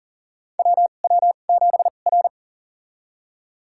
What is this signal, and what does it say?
Morse code, heard as a keyed tone, about 40 wpm, spaced as for 29 wpm, 700 Hz. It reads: WW7R
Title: USA Call Signs - 4 Characters - 2x Character Spacing - 40wpm